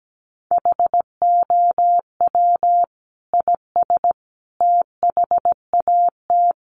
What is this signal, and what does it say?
Morse code, heard as a keyed tone, about 17 wpm, 700 Hz. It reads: HOW IS THAT
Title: Sentences from Top 500 Words 17wpm